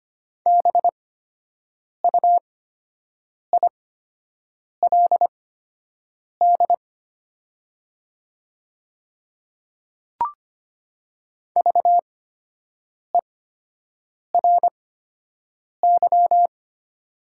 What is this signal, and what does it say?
Morse code, heard as a keyed tone, about 25 wpm, spaced as for 7 wpm, 700 Hz. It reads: BUILD VERY
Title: 8x Character Spacing - Top 200 Words - 25wpm